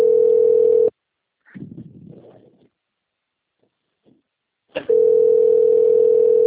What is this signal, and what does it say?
Ringback tone (North America), 440 + 480 Hz.
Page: Jacarei - SP